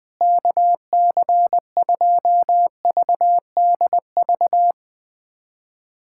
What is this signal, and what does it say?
Morse code, heard as a keyed tone, about 20 wpm, 700 Hz. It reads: KC2VDV
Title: USA Callsign DE Callsign - 6 Characters - 20wpm